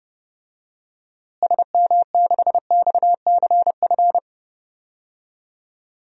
Morse code, heard as a keyed tone, about 30 wpm, 700 Hz. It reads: SM6XCF
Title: DXCC-100 Call Signs - 6 Characters - Repeated 3x - 30wpm